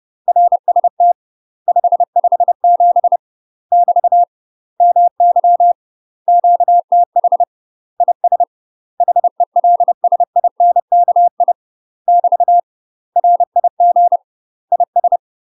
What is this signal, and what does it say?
Morse code, heard as a keyed tone, about 30 wpm, 700 Hz. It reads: RST 557 <BT> MY QTH IS HELSINKI <BT> RIG IS
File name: UK MASTER QSOs TRAINING / 30WPM / UK MASTER QSOs 0008